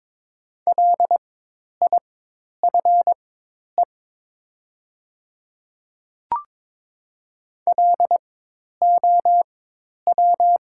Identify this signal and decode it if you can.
Morse code, heard as a keyed tone, about 23 wpm, spaced as for 10 wpm, 700 Hz. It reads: LIFE LOW
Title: 4x Character Spacing - Top 300 Words - 22wpm